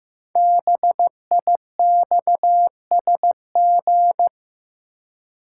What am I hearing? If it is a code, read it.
Morse code, heard as a keyed tone, about 15 wpm, 700 Hz. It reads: BIXSG
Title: Code Groups - 5 Letters - 15wpm